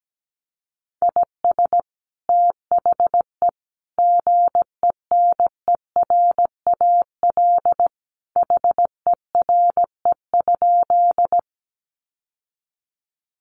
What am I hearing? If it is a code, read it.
Morse code, heard as a keyed tone, about 17 wpm, 700 Hz. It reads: IS THE GENERAL HERE?
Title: Sentences from Top 600 Words 17wpm